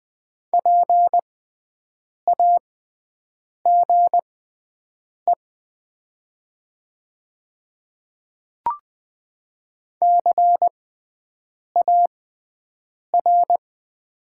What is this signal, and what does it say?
Morse code, heard as a keyed tone, about 20 wpm, spaced as for 7 wpm, 700 Hz. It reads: PAGE CAR
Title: 6x Character Spacing - Top 300 Words - 20wpm